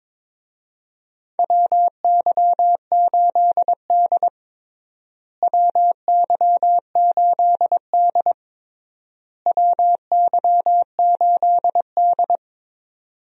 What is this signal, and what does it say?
Morse code, heard as a keyed tone, about 22 wpm, 700 Hz. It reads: WY8D WY8D WY8D
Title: USA Call Signs - 4 Characters - Repeated 3x - 22wpm